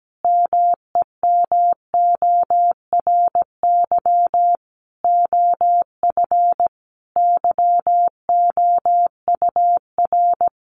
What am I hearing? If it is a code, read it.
Morse code, heard as a keyed tone, about 17 wpm, 700 Hz. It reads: MEMORY OF YOUR
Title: Sentences from Top 1300 Words 17wpm